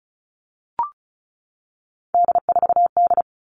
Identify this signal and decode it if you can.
Morse code, heard as a keyed tone, about 34 wpm, 700 Hz. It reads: D4D
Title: All Cut Numbers - 3-Digit 35wpm